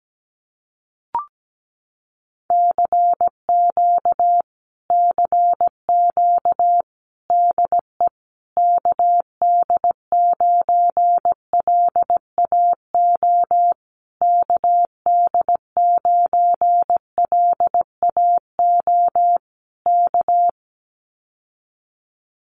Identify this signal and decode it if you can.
Morse code, heard as a keyed tone, about 17 wpm, 700 Hz. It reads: CQ CQ DE KD9LAO KD9LAO K
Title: Common QSO Phrases Encore 17wpm